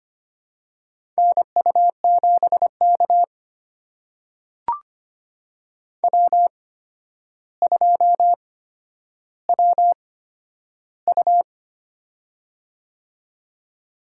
Morse code, heard as a keyed tone, about 25 wpm, spaced as for 11 wpm, 700 Hz. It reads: NU7K W2WU
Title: USA Call Signs - 4 Characters - 8x Character Spacing - 25wpm